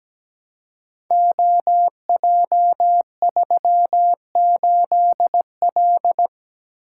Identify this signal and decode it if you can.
Morse code, heard as a keyed tone, about 17 wpm, 700 Hz. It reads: OJ38L
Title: Code Groups - 5 Letter:Numbers - 17wpm